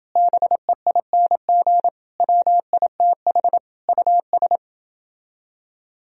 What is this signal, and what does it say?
Morse code, heard as a keyed tone, about 29 wpm, 700 Hz. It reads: BEING WITH US